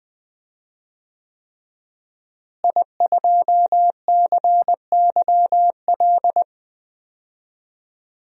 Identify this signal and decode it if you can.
Morse code, heard as a keyed tone, about 21 wpm, 700 Hz. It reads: I2CYL
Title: DXCC-100 Call Signs - 5 Characters - Repeated 3x - 20wpm